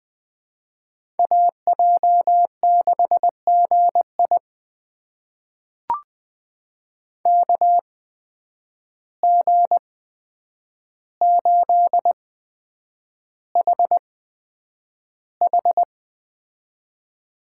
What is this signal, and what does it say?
Morse code, heard as a keyed tone, about 20 wpm, spaced as for 9 wpm, 700 Hz. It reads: AJ6GI KG8HH
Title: USA Call Signs - 5 Characters - 8x Character Spacing - 20wpm